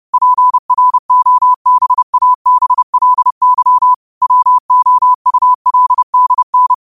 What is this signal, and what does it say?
Morse code, heard as a keyed tone, about 30 wpm, 1000 Hz. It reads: PROBABLY WOULDN
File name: 30wpm